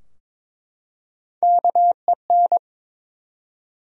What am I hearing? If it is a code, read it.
Morse code, heard as a keyed tone, about 21 wpm, 700 Hz. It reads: KEN
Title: USA Common Names - Max 4 Letters - 22wpm